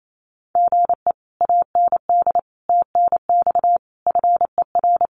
Morse code, heard as a keyed tone, about 28 wpm, 700 Hz. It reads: GE AND TNX FER
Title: Common QSO Phrases Encore 28wpm